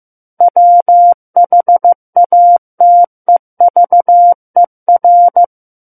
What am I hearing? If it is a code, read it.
Morse code, heard as a keyed tone, about 15 wpm, 700 Hz. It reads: WHATEVER
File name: Chapter 6 - 15 WPM